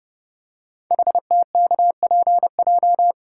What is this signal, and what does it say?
Morse code, heard as a keyed tone, about 30 wpm, 700 Hz. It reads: HTKPJ
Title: Code Groups - 5 Letter:Numbers - 30wpm